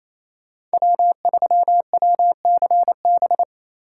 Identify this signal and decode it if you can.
Morse code, heard as a keyed tone, about 36 wpm, 700 Hz. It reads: W3WCB